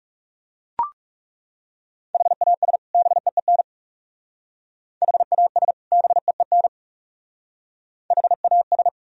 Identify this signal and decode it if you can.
Morse code, heard as a keyed tone, about 36 wpm, 700 Hz. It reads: HAS BEEN HAS BEEN HAS
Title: Speed-Racing - Sets of 2 Words 30wpm